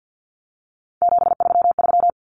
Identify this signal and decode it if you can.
Morse code, heard as a keyed tone, about 50 wpm, 700 Hz. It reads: 733
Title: Common Cut Numbers - 3-Digit 50wpm